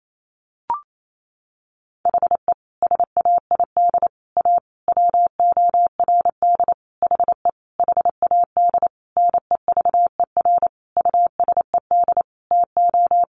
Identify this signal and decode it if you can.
Morse code, heard as a keyed tone, about 28 wpm, 700 Hz. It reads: HE SAID A WORD HE HAD NEVER USED TO
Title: Sentences from Top 600 Words 28wpm